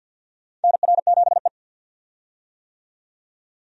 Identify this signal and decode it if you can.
Morse code, heard as a keyed tone, about 53 wpm, 700 Hz. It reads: NR7E
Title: USA Call Signs - 4 Characters - 50wpm